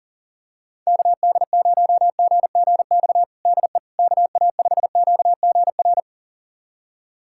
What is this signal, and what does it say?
Morse code, heard as a keyed tone, about 40 wpm, 700 Hz. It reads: KD0GGX DE KA5QGR
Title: USA Callsign DE Callsign 40wpm